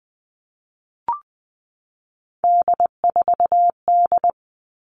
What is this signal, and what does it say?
Morse code, heard as a keyed tone, about 20 wpm, 700 Hz. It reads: D4D
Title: All Cut Numbers - 3-Digit 20wpm